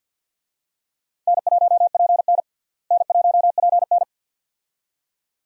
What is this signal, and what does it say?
Morse code, heard as a keyed tone, about 47 wpm, 700 Hz. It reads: N1PN N1PN
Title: USA Call Signs - 4 Characters - Repeated 2x - 50wpm